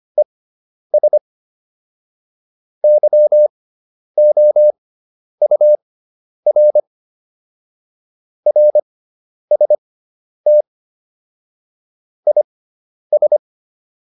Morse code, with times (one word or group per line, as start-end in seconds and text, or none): 0.2-1.2 ES
2.8-6.8 YOUR
8.5-10.6 RST
12.3-13.4 IS